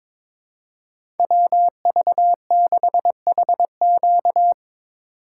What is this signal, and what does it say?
Morse code, heard as a keyed tone, about 22 wpm, 700 Hz. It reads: WV6HQ
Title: USA Call Signs - 5 Characters - 6x Character Spacing - 22wpm